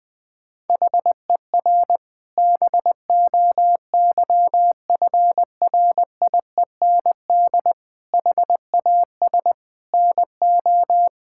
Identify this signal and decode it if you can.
Morse code, heard as a keyed tone, about 20 wpm, 700 Hz. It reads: HER BOYFRIEND HAS NO
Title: Sentences from Top 1000 Words - Review Part 5 - 20wpm